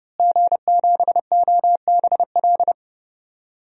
Morse code, heard as a keyed tone, about 29 wpm, 700 Hz. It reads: G7OBL
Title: DXCC-100 Call Signs - 5 Characters - Repeated 3x - 30wpm